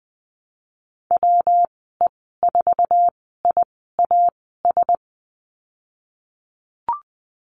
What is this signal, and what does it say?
Morse code, heard as a keyed tone, about 19 wpm, spaced as for 14 wpm, 700 Hz. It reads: WE4IAS
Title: 147 Lesson 24g - Letter y - Call Signs 20wpm